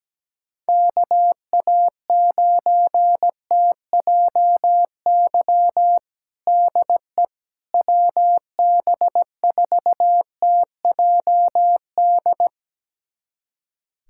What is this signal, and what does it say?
Morse code, heard as a keyed tone, about 17 wpm, 700 Hz. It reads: KA9TJY DE WB4TJD
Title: USA Callsign DE Callsign 17wpm